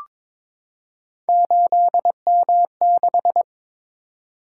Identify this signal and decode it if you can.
Morse code, heard as a keyed tone, about 22 wpm, 700 Hz. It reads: 8M6
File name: practice / 22wpm / Three Letter-Number Combinations - 22wpm